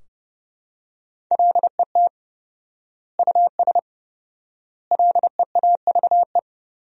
Morse code, heard as a keyed tone, about 30 wpm, 700 Hz. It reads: LET US LEAVE
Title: 4x Word-Spacing - Sentences from Top 400 Words - 30wpm